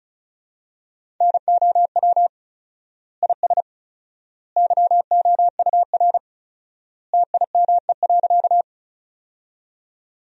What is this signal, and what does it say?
Morse code, heard as a keyed tone, about 36 wpm, 700 Hz. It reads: NOW IS YOUR TIME.
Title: 4x Word-Spacing - Sentences from Top 100 Words - 35wpm